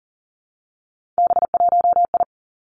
Morse code, heard as a keyed tone, about 40 wpm, 700 Hz. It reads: B1I
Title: Three Challenging Letter-Number Combinations 40wpm